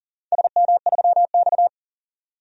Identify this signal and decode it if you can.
Morse code, heard as a keyed tone, about 39 wpm, 700 Hz. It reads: SM3X
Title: DXCC-100 Call Signs - 4 Characters - 40wpm